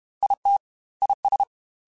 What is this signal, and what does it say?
Morse code, heard as a keyed tone, about 32 wpm, 800 Hz. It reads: IT IS
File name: Technical Subjects / Technical Subjects-mp3s / AND Gate @30